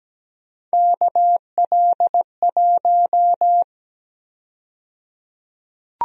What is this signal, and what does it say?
Morse code, heard as a keyed tone, about 17 wpm, 700 Hz. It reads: KL1
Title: USA Call Sign Prefixes - 17wpm